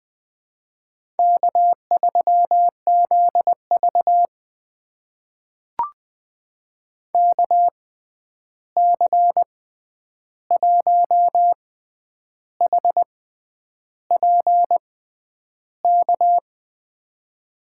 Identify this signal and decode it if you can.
Morse code, heard as a keyed tone, about 20 wpm, spaced as for 9 wpm, 700 Hz. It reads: K3ZV KC1HPK
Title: USA Call Signs - 6x Character Spacing - 20wpm